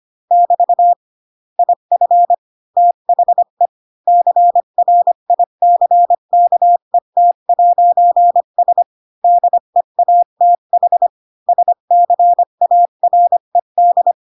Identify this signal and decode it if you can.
Morse code, heard as a keyed tone, about 25 wpm, 700 Hz. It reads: <BT> IF THE CRICKET'S DEATH SCARED
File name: Pinocchio / Pinocchio ch5 20 25 ik1hge